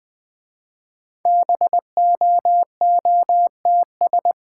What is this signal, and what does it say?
Morse code, heard as a keyed tone, about 20 wpm, 700 Hz. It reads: BOOTS
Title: Easy Words with Hints - 20wpm